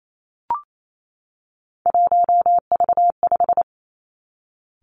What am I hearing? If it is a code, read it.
Morse code, heard as a keyed tone, about 28 wpm, 700 Hz. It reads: 1V5
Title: Three Challenging Letter-Number Combinations 28wpm